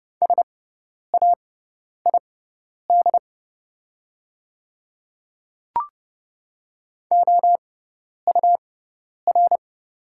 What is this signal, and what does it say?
Morse code, heard as a keyed tone, about 30 wpm, spaced as for 10 wpm, 700 Hz. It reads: SAID OUR
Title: 6x Character Spacing - Top 500 Words - 30wpm